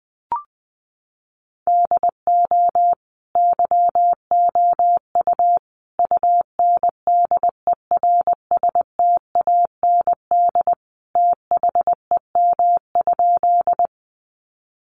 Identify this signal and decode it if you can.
Morse code, heard as a keyed tone, about 20 wpm, 700 Hz. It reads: DO YOU UNDERSTAND THEM?
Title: Sentences from Top 600 Words 20wpm